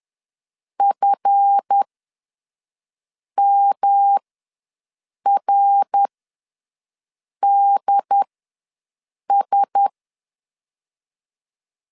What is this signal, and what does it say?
Morse code, heard as a keyed tone, about 10 wpm, spaced as for 5 wpm, 800 Hz. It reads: FMRDS
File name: LEZIONI / 7 GF9Z / 7 GF ws 10wpm 3918